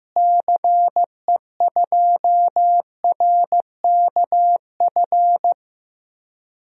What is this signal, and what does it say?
Morse code, heard as a keyed tone, about 16 wpm, 700 Hz. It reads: CE2RKF
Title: DXCC-100 Call Signs - 6 Characters - Repeated 2x - 15wpm